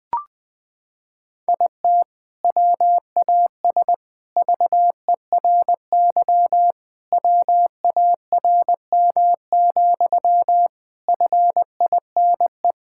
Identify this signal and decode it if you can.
Morse code, heard as a keyed tone, about 21 wpm, 700 Hz. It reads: IT WAS VERY WARM, FINE
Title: Sentences from Top 1000 Words - Review Part 5 - 20wpm